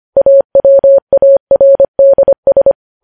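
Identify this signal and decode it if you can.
Morse code, heard as a keyed tone, about 25 wpm, 550 Hz. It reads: AWARDS
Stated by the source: Morse code news headlines sent at 25 wpm.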